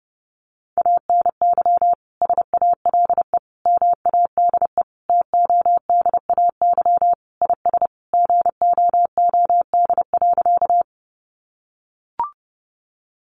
Morse code, heard as a keyed tone, about 30 wpm, 700 Hz. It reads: ANY SALE MADE TODAY IS GOOD.
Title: Sentences from Top 1400 Words 30wpm